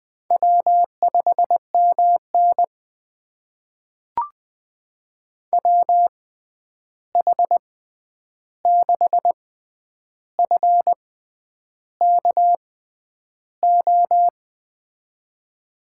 Morse code, heard as a keyed tone, about 21 wpm, spaced as for 9 wpm, 700 Hz. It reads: W5MN WH6FKO
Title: USA Call Signs - 6x Character Spacing - 20wpm